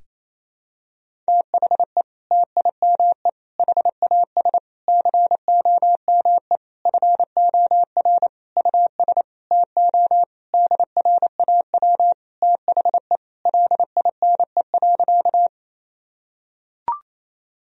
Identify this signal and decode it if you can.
Morse code, heard as a keyed tone, about 28 wpm, 700 Hz. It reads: THE TIME HAS COME FOR US TO DRAW THE LINE.
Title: Sentences from Top 1000 Words - Review Part 1 - 28wpm